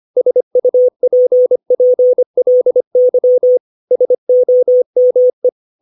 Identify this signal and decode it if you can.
Morse code, heard as a keyed tone, about 25 wpm, 500 Hz. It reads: SUPPLY SOME
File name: 632 words / 632 Words 25WPM